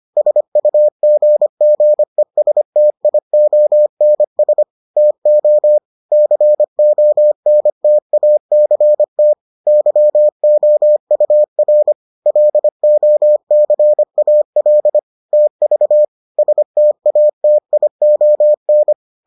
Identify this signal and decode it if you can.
Morse code, heard as a keyed tone, about 25 wpm, 600 Hz. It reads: SUGGESTIONS TO CONTACT YOUR LOCAL TV STATION